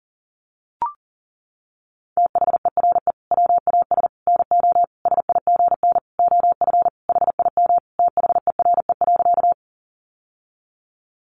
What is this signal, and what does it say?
Morse code, heard as a keyed tone, about 37 wpm, 700 Hz. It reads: THERE WAS NO SIGN OF HIM THERE.
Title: Sentences from Top 1000 Words 40wpm